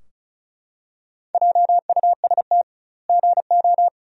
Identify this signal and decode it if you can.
Morse code, heard as a keyed tone, about 34 wpm, 700 Hz. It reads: JUST GO
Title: 2x Word-Spacing - Sentences from Top 500 Words - 35wpm